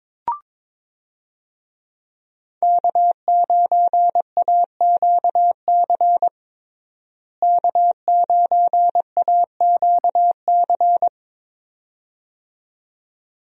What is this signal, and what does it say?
Morse code, heard as a keyed tone, about 22 wpm, 700 Hz. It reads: K9AQC K9AQC
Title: USA Call Signs - 5 Characters - Repeated 2x - 22wpm